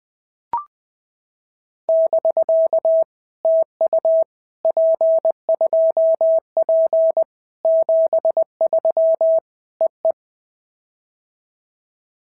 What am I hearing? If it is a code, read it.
Morse code, heard as a keyed tone, about 19 wpm, 650 Hz. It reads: <BK> TU P2P 73 EE
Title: POTA QSO Practice - 20wpm